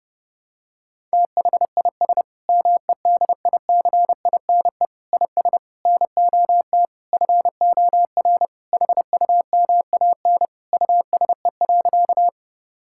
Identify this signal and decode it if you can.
Morse code, heard as a keyed tone, about 30 wpm, 700 Hz. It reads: THIS MEDICINE IS NOT FOR HUMAN USE.
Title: Sentences from Top 1000 Words - Review Part 6 - 30wpm